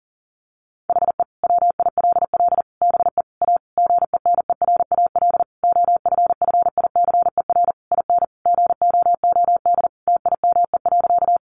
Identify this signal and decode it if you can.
Morse code, heard as a keyed tone, about 40 wpm, 700 Hz. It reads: HE WILL BE A GENERAL OFFICER IN GOOD TIME.
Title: Sentences from Top 1200 Words 40wpm